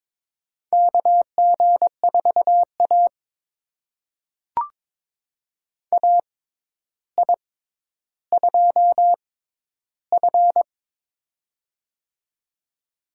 Morse code, heard as a keyed tone, about 22 wpm, spaced as for 11 wpm, 700 Hz. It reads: KG4A AI2F
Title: USA Call Signs - 4 Characters - 6x Character Spacing - 22wpm